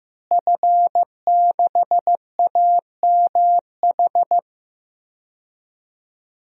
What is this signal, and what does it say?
Morse code, heard as a keyed tone, about 15 wpm, 700 Hz. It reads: F6AMH
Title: DXCC-100 Call Signs - 5 Characters - Repeated 2x - 15wpm